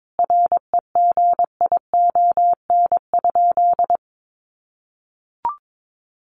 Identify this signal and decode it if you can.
Morse code, heard as a keyed tone, about 22 wpm, 700 Hz. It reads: REGION?
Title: Sentences from Top 1500 Words 22wpm